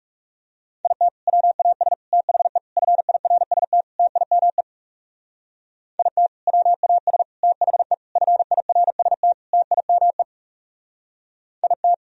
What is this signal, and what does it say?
Morse code, heard as a keyed tone, about 36 wpm, 700 Hz. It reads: IT WAS THE FIRST TIME IT WAS THE FIRST TIME IT
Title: Speed-Racing - Sets of 5 Words 30wpm